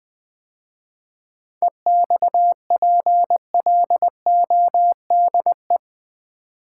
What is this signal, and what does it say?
Morse code, heard as a keyed tone, about 20 wpm, 700 Hz. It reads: EXPLODE
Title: 7 Letter Words - First 3 Letters Given - 20wpm